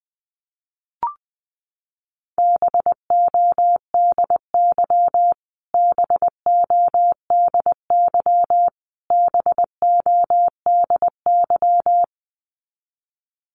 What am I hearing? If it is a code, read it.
Morse code, heard as a keyed tone, about 20 wpm, 700 Hz. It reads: BODY BODY BODY
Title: Top 500 Words Repeated 20wpm